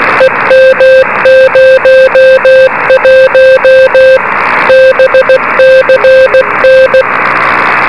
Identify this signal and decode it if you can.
Morse code, heard as a keyed tone, about 16 wpm, 520 Hz, with noise. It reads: EM01 BCN
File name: beacons / k5ab